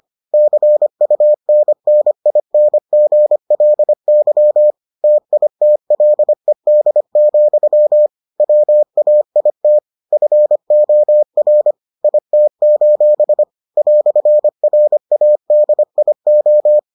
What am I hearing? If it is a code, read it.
Morse code, heard as a keyed tone, about 25 wpm, 600 Hz. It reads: CUNNINGLY TITLED, WAIT FOR IT: "RADIO